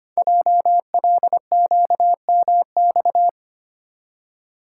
Morse code, heard as a keyed tone, about 25 wpm, 700 Hz. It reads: JLQMX
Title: Code Groups - 5 Letters - 25wpm